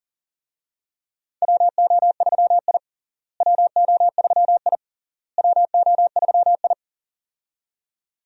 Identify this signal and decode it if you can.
Morse code, heard as a keyed tone, about 40 wpm, 700 Hz. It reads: WO3I WO3I WO3I
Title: USA Call Signs - 4 Characters - Repeated 3x - 40wpm